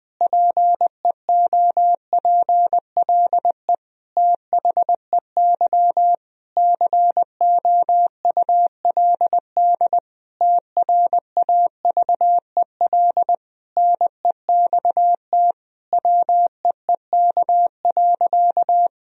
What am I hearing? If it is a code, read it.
Morse code, heard as a keyed tone, about 21 wpm, 700 Hz. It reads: PEOPLE THEY COULD TRAVEL NEXT WEEK.
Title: Sentences from Top 1000 Words - Review Part 2 - 20wpm